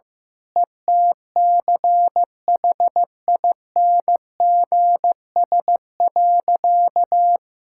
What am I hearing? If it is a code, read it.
Morse code, heard as a keyed tone, about 15 wpm, 700 Hz. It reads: ETCHINGS.
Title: Pangram Challenge v2 - 15wpm